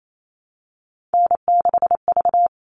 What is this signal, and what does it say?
Morse code, heard as a keyed tone, about 28 wpm, 700 Hz. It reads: N6V
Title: All Cut Numbers - 3-Digit 28wpm